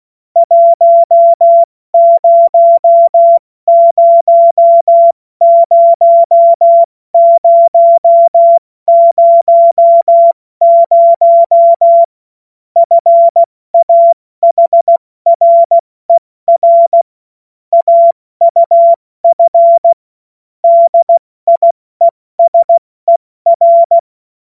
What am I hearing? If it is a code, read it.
Morse code, heard as a keyed tone, about 16 wpm, 680 Hz. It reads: 1000000 FAHRER AUF DIESER
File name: Morsekurs / Witze / Witz 26. Oktober 2021, 70 F80